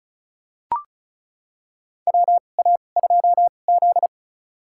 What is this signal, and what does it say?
Morse code, heard as a keyed tone, about 36 wpm, spaced as for 25 wpm, 700 Hz. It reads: WA2Z
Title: USA Call Signs - 4 Characters - 2x Character Spacing - 35wpm